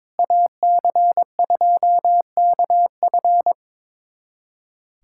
Morse code, heard as a keyed tone, about 22 wpm, 700 Hz. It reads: AC2KF
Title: USA Callsign DE Callsign - 5 Characters - 22wpm